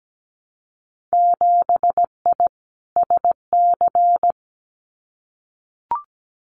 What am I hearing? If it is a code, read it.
Morse code, heard as a keyed tone, about 17 wpm, 700 Hz. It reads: 7I SC
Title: Winter Field Day Exchanges Encore 17wpm